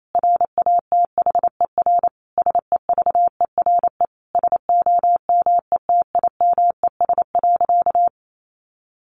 Morse code, heard as a keyed tone, about 28 wpm, 700 Hz. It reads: RATHER SEVERE SOMETIMES.
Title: Sentences from Top 1400 Words 28wpm